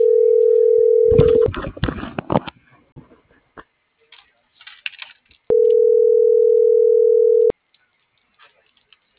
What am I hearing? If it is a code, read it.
Ringback tone (North America), 440 + 480 Hz.